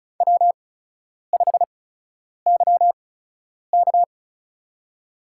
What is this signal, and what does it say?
Morse code, heard as a keyed tone, about 36 wpm, spaced as for 10 wpm, 700 Hz. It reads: W5YK
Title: USA Call Signs - 8x Character Spacing - 35wpm